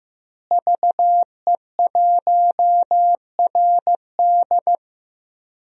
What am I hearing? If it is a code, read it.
Morse code, heard as a keyed tone, about 15 wpm, 700 Hz. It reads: VE1RD
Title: DXCC-100 Call Signs - 5 Characters - Repeated 3x - 15wpm